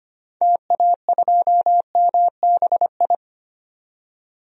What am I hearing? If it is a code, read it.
Morse code, heard as a keyed tone, about 25 wpm, 700 Hz. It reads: TA2MBI
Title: DXCC-100 Call Signs - 6 Characters - 25wpm